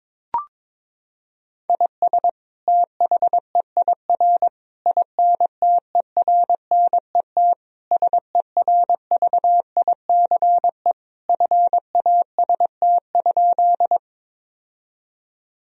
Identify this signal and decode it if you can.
Morse code, heard as a keyed tone, about 22 wpm, 700 Hz. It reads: IS THEIR INTERNET SERVICE FAST?
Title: Sentences from Top 1000 Words - Review Part 4 - 22wpm